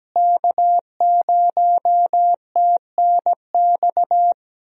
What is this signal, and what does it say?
Morse code, heard as a keyed tone, about 17 wpm, 700 Hz. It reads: K0TNX
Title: USA Call Signs - 5 Characters - Repeated 3x - 17wpm